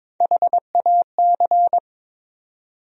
Morse code, heard as a keyed tone, about 22 wpm, 700 Hz. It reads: HAC
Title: USA Call Sign Postfixes - 3 Letters 22wpm